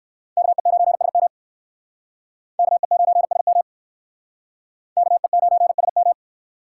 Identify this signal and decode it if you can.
Morse code, heard as a keyed tone, about 57 wpm, 700 Hz. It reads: KE9SM KE9SM KE9SM